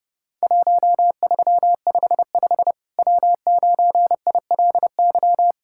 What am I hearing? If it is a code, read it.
Morse code, heard as a keyed tone, about 30 wpm, 700 Hz. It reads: 1355 W9ILY
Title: Worked All Europe DX Contest - QTC Exchange - 30wpm